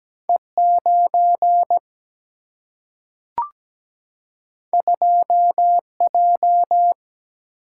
Morse code, heard as a keyed tone, about 17 wpm, 700 Hz. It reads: E9 2J
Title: Single-Letter Single-Number 17wpm